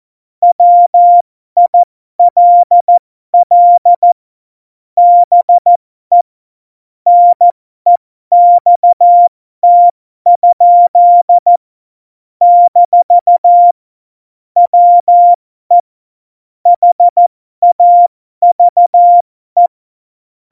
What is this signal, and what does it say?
Morse code, heard as a keyed tone, about 14 wpm, spaced as for 12 wpm, 700 Hz. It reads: WILL BE NEXT? - WE HAVE
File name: Morsekurs / Witze